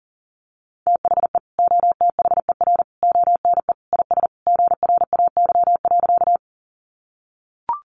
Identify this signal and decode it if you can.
Morse code, heard as a keyed tone, about 40 wpm, 700 Hz. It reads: THE OTHER ONE IS GRAY.
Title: Additional Easy Phrases 40wpm